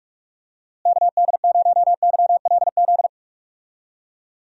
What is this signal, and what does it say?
Morse code, heard as a keyed tone, about 44 wpm, 700 Hz. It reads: KD0YLZ
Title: USA Call Signs - 6 Characters - Repeated 2x - 45wpm